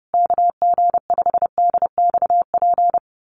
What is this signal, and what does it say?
Morse code, heard as a keyed tone, about 30 wpm, 700 Hz. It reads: KG5DXP
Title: USA Call Signs Repeated 3x 30wpm